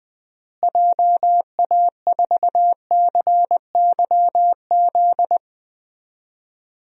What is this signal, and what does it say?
Morse code, heard as a keyed tone, about 20 wpm, 700 Hz. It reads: JA4CYZ
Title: DXCC-100 Call Signs - 6 Characters - Repeated 3x - 20wpm